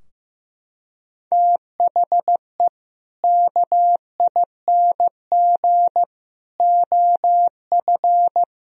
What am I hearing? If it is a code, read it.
Morse code, heard as a keyed tone, about 15 wpm, 700 Hz. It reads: THE KING OF
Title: Sentences from Top 1000 Words - Review Part 2 - 15wpm